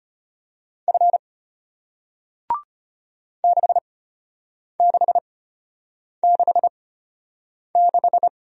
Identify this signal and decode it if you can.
Morse code, heard as a keyed tone, about 33 wpm, spaced as for 8 wpm, 700 Hz. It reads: F 6666